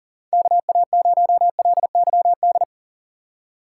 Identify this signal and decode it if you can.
Morse code, heard as a keyed tone, about 40 wpm, 700 Hz. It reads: KA0LYD
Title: USA Call Signs - 6 Characters - 2x Character Spacing - 40wpm